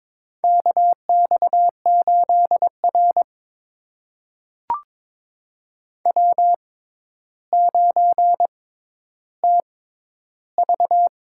Morse code, heard as a keyed tone, about 23 wpm, spaced as for 11 wpm, 700 Hz. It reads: KX8R W9TV
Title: USA Call Signs - 4 Characters - 6x Character Spacing - 22wpm